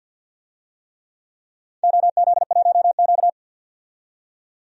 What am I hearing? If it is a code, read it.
Morse code, heard as a keyed tone, about 52 wpm, 700 Hz. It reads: OZ1Q
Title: DXCC-100 Call Signs - 4 Characters - 50wpm